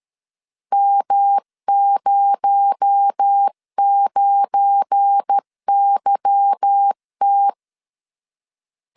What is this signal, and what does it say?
Morse code, heard as a keyed tone, about 13 wpm, 800 Hz. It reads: M09YT
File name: CW / DET / 1 9 1500C S A 12wpm 3800